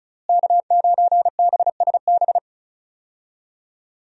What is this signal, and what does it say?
Morse code, heard as a keyed tone, about 34 wpm, 700 Hz. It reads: K9BSB
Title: USA Call Signs - 5 Characters - Repeated 3x - 35wpm